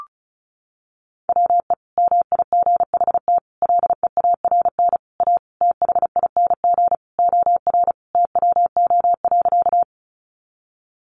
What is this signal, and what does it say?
Morse code, heard as a keyed tone, about 34 wpm, 700 Hz. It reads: WE MIGHT LEARN A THING OR TWO.
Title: Sentences from Top 300 Words 35wpm